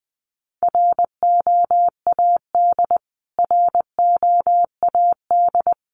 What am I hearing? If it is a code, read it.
Morse code, heard as a keyed tone, about 19 wpm, 700 Hz. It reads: ROAD ROAD
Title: Top 500 Words Repeated 20wpm